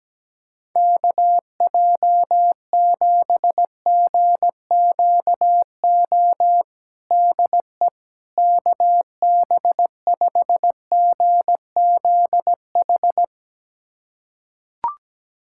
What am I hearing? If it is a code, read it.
Morse code, heard as a keyed tone, about 17 wpm, 700 Hz. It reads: KJ7GQO DE KB5GZH
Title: USA Callsign DE Callsign - 6 Characters - 17wpm